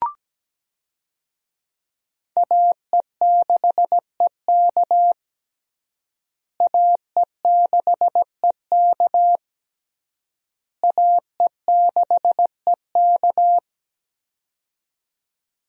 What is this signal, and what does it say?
Morse code, heard as a keyed tone, about 17 wpm, 700 Hz. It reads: AE6EK AE6EK AE6EK
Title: USA Call Signs - 5 Characters - Repeated 3x - 17wpm